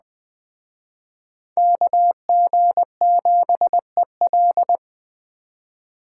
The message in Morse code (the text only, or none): KG7EL